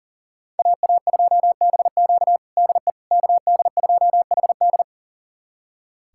Morse code, heard as a keyed tone, about 40 wpm, 700 Hz. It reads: AA2BQ DE KD2HD
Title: USA Callsign DE Callsign - 5 Characters - 40wpm